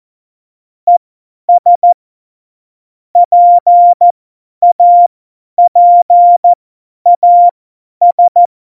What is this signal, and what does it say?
Morse code, heard as a keyed tone, about 14 wpm, spaced as for 10 wpm, 700 Hz. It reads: ES PAPAS
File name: Morsekurs / Witze / Witz 21. Mai 2019, 50 F70